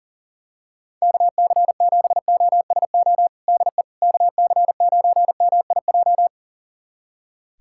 Morse code, heard as a keyed tone, about 40 wpm, 700 Hz. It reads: KC7OSO DE KC9MIJ
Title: USA Callsign DE Callsign - 6 Characters - 40wpm